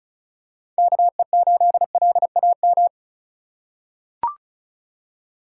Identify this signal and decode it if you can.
Morse code, heard as a keyed tone, about 36 wpm, 700 Hz. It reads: KE8LAM
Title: USA Call Signs - 6 Characters - 4x Character Spacing - 35wpm